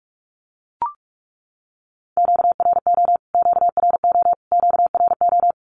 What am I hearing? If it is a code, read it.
Morse code, heard as a keyed tone, about 44 wpm, 700 Hz. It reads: QRO QRO QRO
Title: Q-Codes Repeated 3x 45wpm